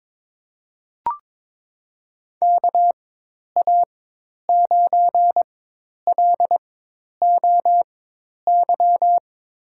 Morse code, heard as a keyed tone, about 22 wpm, spaced as for 10 wpm, 700 Hz. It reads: KA9LOY
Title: USA Call Signs - 6 Characters - 4x Character Spacing - 22wpm